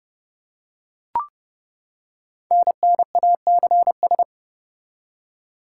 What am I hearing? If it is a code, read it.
Morse code, heard as a keyed tone, about 30 wpm, 700 Hz. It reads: NNACS
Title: Code Groups - 5 Letters - 30wpm